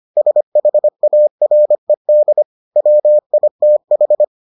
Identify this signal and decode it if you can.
Morse code, heard as a keyed tone, about 25 wpm, 600 Hz. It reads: SHARED WITH